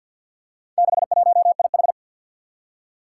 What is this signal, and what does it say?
Morse code, heard as a keyed tone, about 45 wpm, 700 Hz. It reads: 61IH
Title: Four Challenging Letter-Number Combinations 50wpm